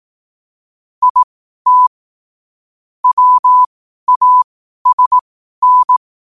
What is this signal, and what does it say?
Morse code, heard as a keyed tone, about 18 wpm, spaced as for 12 wpm, 1000 Hz. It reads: IT WASN
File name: qotdfiles / 12wpm